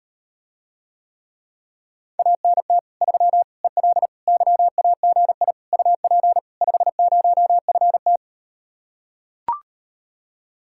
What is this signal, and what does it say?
Morse code, heard as a keyed tone, about 38 wpm, 700 Hz. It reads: ANT 3 EL YAGI UP 50FT